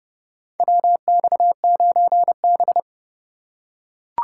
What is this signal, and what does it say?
Morse code, heard as a keyed tone, about 31 wpm, 700 Hz. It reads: WX9B
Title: USA Call Signs - 4 Characters - 3x Character Spacing - 30wpm